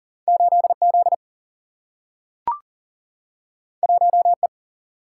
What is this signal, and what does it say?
Morse code, heard as a keyed tone, about 39 wpm, 700 Hz. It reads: 8Z 1E
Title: Single-Letter Single-Number 40wpm